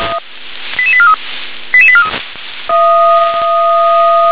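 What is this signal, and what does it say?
Five-tone selective call (ZVEI), heard as quick runs of five tones, tones of about 70 ms. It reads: 88843 88843
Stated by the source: Five-tone ZVEI selective call received on SDR.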